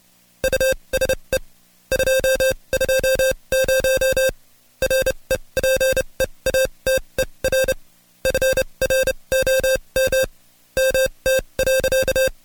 Morse code, heard as a keyed tone, about 29 wpm, spaced as for 23 wpm, 530 Hz. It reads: USE 220 REPEATER FROM MT.
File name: Code Quick / AudioDownld2 / Seg138-Stepper11 23WPM